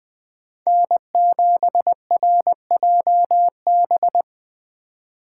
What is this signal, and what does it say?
Morse code, heard as a keyed tone, about 21 wpm, 700 Hz. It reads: N7RJB
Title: Speed Racing - USA 1x3 Call Signs 17wpm